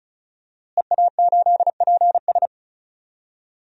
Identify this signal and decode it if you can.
Morse code, heard as a keyed tone, about 34 wpm, 690 Hz. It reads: EA8PS
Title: DXCC-100 Call Signs - 5 Characters - Repeated 3x - 35wpm